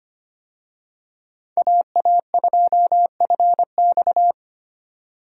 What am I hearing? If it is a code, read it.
Morse code, heard as a keyed tone, about 25 wpm, 700 Hz. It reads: AA2FX
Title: USA Call Signs - 5 Characters - 25wpm